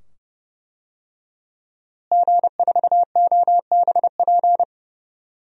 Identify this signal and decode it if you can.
Morse code, heard as a keyed tone, about 31 wpm, 700 Hz. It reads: G4OBP
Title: DXCC-100 Call Signs - 5 Characters - Repeated 2x - 30wpm